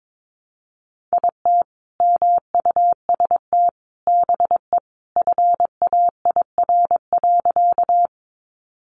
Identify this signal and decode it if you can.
Morse code, heard as a keyed tone, about 22 wpm, 700 Hz. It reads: IT MUST BE FAIR.
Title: Sentences from Top 1300 Words 22wpm